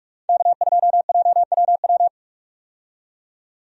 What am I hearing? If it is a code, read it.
Morse code, heard as a keyed tone, about 47 wpm, 700 Hz. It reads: K2JWW
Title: USA Call Signs - 5 Characters - 45wpm